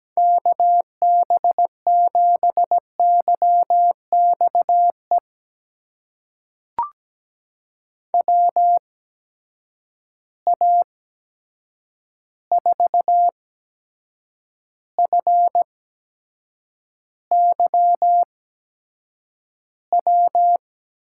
Morse code, heard as a keyed tone, about 17 wpm, spaced as for 7 wpm, 700 Hz. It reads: KB7YXE WA4FYW
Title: USA Call Signs - 6 Characters - 8x Character Spacing - 17wpm